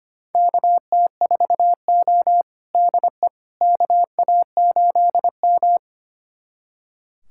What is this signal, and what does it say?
Morse code, heard as a keyed tone, about 25 wpm, 700 Hz. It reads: KT4O DE KA8M
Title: USA Callsign DE Callsign - 4 Characters - 25wpm